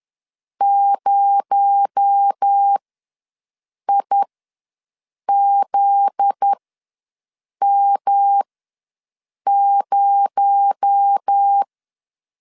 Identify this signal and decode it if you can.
Morse code, heard as a keyed tone, about 10 wpm, spaced as for 6 wpm, 800 Hz. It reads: 0IZM0
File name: LEZIONI / 7 GF9Z / 7 GF9Z ws 10wpm 3919